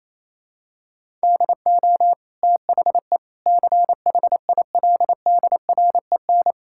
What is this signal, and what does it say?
Morse code, heard as a keyed tone, about 28 wpm, 700 Hz. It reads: DO THE CHILDREN
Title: Sentences from Top 1000 Words - Review Part 6 - 28wpm